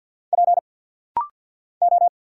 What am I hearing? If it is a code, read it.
Morse code, heard as a keyed tone, about 50 wpm, 690 Hz. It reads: P O
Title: Single Letters - Rapid-Fire 50wpm